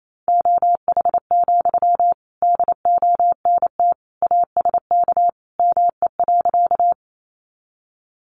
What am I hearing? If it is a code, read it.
Morse code, heard as a keyed tone, about 28 wpm, 700 Hz. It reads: OH, DONT ASK ME.
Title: Sentences from Top 500 Words 28wpm